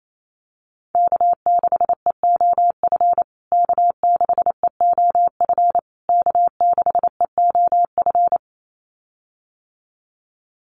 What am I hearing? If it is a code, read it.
Morse code, heard as a keyed tone, about 28 wpm, 700 Hz. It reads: K6EOF K6EOF K6EOF
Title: USA Call Signs Repeated 3x 28wpm